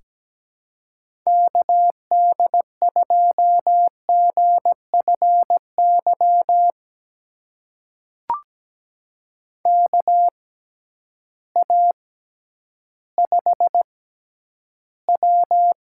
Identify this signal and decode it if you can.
Morse code, heard as a keyed tone, about 17 wpm, spaced as for 10 wpm, 700 Hz. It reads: KD2GFY KA5W
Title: USA Call Signs - 6 Characters - 6x Character Spacing - 17wpm